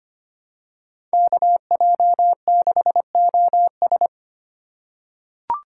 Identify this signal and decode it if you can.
Morse code, heard as a keyed tone, about 25 wpm, 700 Hz. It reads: KJ6OS
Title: USA Call Signs - 2x Character Spacing - 25wpm